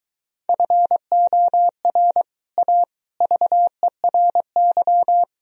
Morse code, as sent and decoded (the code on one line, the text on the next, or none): ..-. --- .-. / .- / ...- . .-. -.--
FOR A VERY